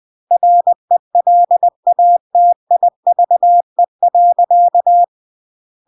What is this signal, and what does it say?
Morse code, heard as a keyed tone, about 20 wpm, 700 Hz. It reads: RELATIVE.
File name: CW / jodorowsky / 82 commandments 20-20